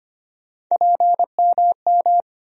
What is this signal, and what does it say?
Morse code, heard as a keyed tone, about 25 wpm, 700 Hz. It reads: PMM
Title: USA Call Sign Postfixes - 3 Letters 25wpm